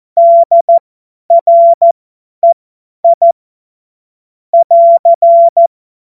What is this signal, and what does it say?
Morse code, heard as a keyed tone, about 14 wpm, spaced as for 10 wpm, 680 Hz. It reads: DREI <AR>
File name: Morsekurs / Witze / Witz 02 Januar 2024 50 F70